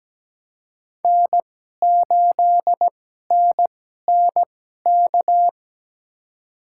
Morse code, heard as a keyed tone, about 17 wpm, spaced as for 12 wpm, 700 Hz. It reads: N8NNK
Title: USA Call Signs - 5 Characters - 2x Character Spacing - 17wpm